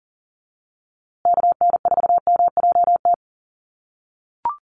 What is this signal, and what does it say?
Morse code, heard as a keyed tone, about 40 wpm, 700 Hz. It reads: KN4MJT
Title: USA Call Signs Repeated 3x 40wpm